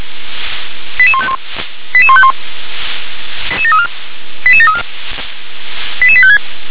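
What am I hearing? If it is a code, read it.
Five-tone selective call (ZVEI), heard as quick runs of five tones, tones of about 70 ms. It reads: 88151 88151 88734 88856
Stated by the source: ZVEI five-tone selective call received on SDR.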